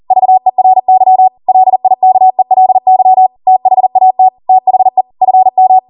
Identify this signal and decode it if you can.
Morse code, heard as a keyed tone, about 40 wpm, 750 Hz. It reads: VERY LIKELY THAT THE FM
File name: Morse / 40WPM